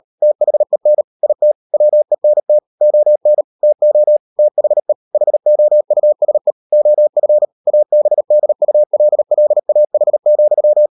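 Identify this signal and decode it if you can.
Morse code, heard as a keyed tone, about 38 wpm, 600 Hz. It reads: THEN IT WENT ON TO THE HOUSE OF ABDULLAH,
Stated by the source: Morse code practice at 35–39 wpm sending English text of Ken Follett's novel Never, from CW med Gnister.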